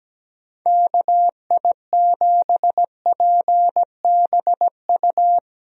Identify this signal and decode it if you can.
Morse code, heard as a keyed tone, about 17 wpm, 700 Hz. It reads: KI7PBU
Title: USA Call Signs - 6 Characters - Repeated 3x - 17wpm